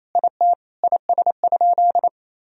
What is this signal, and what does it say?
Morse code, heard as a keyed tone, about 28 wpm, 700 Hz. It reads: IT IS?
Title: Sentences from Top 1000 Words - Review Part 1 - 28wpm